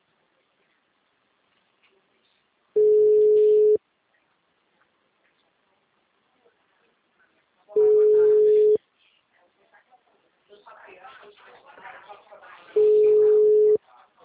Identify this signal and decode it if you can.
Ringback tone (Europe, 425 Hz), with speech.
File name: out- -2007- - - .76273